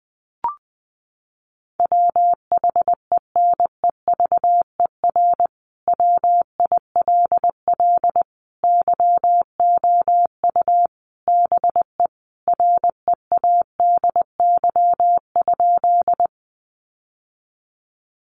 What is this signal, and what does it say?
Morse code, heard as a keyed tone, about 20 wpm, 700 Hz. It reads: WHENEVER WILL YOU BE READY?
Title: Sentences from Top 1200 Words 20wpm